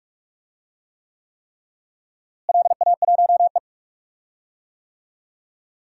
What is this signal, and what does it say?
Morse code, heard as a keyed tone, about 44 wpm, 700 Hz. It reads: LA1E
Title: DXCC-100 Call Signs - 4 Characters - Repeated 3x - 45wpm